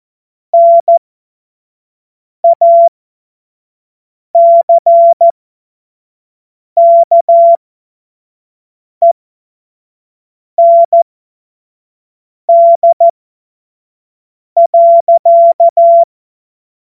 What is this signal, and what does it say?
Morse code, heard as a keyed tone, about 14 wpm, spaced as for 5 wpm, 680 Hz. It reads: NACKEND.